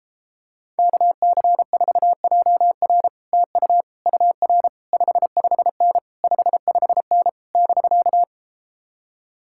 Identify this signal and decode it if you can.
Morse code, heard as a keyed tone, about 33 wpm, 700 Hz. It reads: KC4JR TU UR 55N 55N <BK>